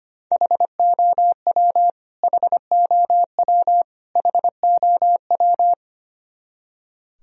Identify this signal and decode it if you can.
Morse code, heard as a keyed tone, about 25 wpm, 700 Hz. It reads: HOW HOW HOW
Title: Top 100 Words Repeated 25wpm